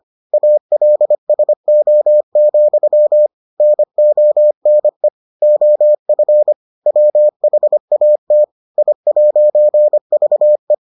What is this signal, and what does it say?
Morse code, heard as a keyed tone, about 25 wpm, 600 Hz. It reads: ALSO, NONE OF WHAT I'VE